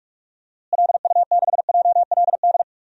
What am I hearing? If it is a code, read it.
Morse code, heard as a keyed tone, about 46 wpm, 700 Hz. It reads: LU6JLD
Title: DXCC-100 Call Signs - 6 Characters - Repeated 3x - 45wpm